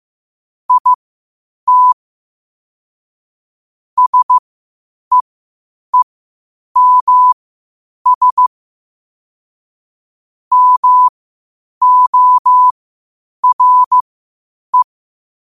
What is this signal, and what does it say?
Morse code, heard as a keyed tone, about 15 wpm, spaced as for 8 wpm, 1000 Hz. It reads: IT SEEMS MORE
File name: qotdfiles / 8wpm